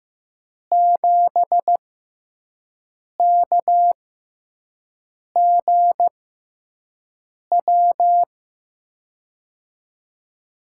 Morse code, heard as a keyed tone, about 15 wpm, spaced as for 5 wpm, 700 Hz. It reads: 7KGW
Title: USA Call Signs - 6 Characters - 6x Character Spacing - 15wpm